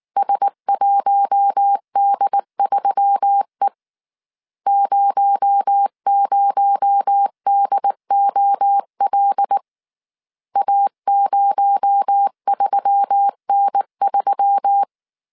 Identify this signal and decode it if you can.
Morse code, heard as a keyed tone, about 19 wpm, 800 Hz. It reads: S1D3E 00DOL A03N3
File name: CW / DET / 1 6 1500C S C 18wpm 2356